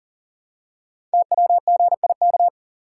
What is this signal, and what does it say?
Morse code, heard as a keyed tone, about 42 wpm, 690 Hz. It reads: TWGIK
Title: Code Groups - 5 Letters - 40wpm